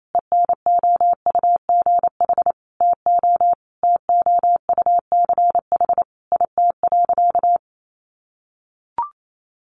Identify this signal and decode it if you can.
Morse code, heard as a keyed tone, about 28 wpm, 700 Hz. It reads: ENOUGH TO TOUCH IT.
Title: Sentences from Top 1200 Words 28wpm